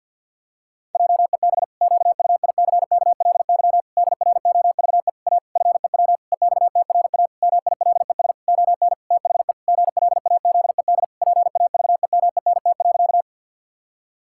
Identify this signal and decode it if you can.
Morse code, heard as a keyed tone, about 50 wpm, 700 Hz. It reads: JEB QUICKLY DROVE A FEW EXTRA MILES ON THE GLAZED PAVEMENT.
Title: Pangram Challenge v2 - 50wpm